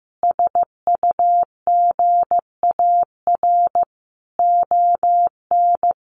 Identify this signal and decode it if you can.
Morse code, heard as a keyed tone, about 15 wpm, 700 Hz. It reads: SUGAR ON
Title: Sentences from Top 1200 Words 15wpm